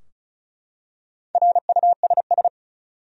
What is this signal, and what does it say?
Morse code, heard as a keyed tone, about 36 wpm, 700 Hz. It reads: RUSS
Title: USA Common Names - Max 5 Letters - 35wpm